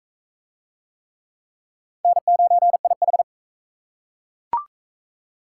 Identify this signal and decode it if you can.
Morse code, heard as a keyed tone, about 36 wpm, 700 Hz. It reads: N9IH